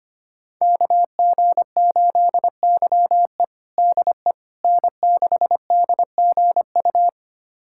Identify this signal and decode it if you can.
Morse code, heard as a keyed tone, about 25 wpm, 700 Hz. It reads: KG8YE DE N6DGU
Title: USA Callsign DE Callsign - 5 Characters - 25wpm